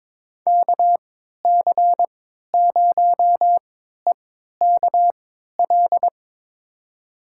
Morse code, heard as a keyed tone, about 22 wpm, spaced as for 13 wpm, 700 Hz. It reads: KC0EKL
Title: USA Call Signs - 6 Characters - 3x Character Spacing - 22wpm